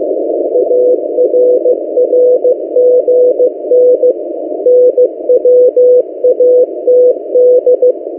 Morse code, heard as a keyed tone, about 15 wpm, 500 Hz. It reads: ARRGN NWATD
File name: M08a- -1355utc-8096khz